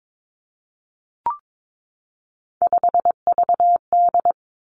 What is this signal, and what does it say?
Morse code, heard as a keyed tone, about 22 wpm, 700 Hz. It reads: 5VD
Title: Three Challenging Letter-Number Combinations 22wpm